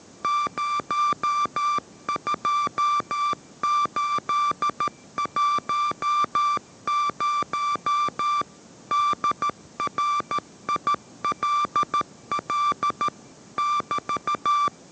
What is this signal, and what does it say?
Morse code, heard as a keyed tone, about 13 wpm, 1170 Hz. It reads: 02810 DRILL <BT>
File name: 14wpm